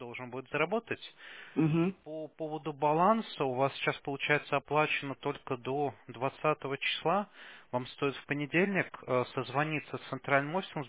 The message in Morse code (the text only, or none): none